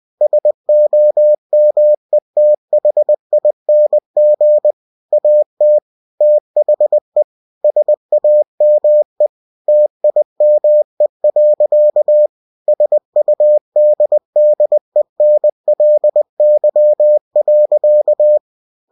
Morse code, heard as a keyed tone, about 20 wpm, 600 Hz. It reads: SOMETHING AT THE SAME TIME. SUDDENLY.
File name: cw-books / winnie-the-pooh / winnie-20wpm-page-0070